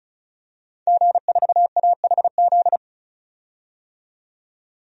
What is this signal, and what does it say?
Morse code, heard as a keyed tone, about 36 wpm, 700 Hz. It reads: G4AHZ
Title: DXCC-100 Call Signs - 5 Characters - 35wpm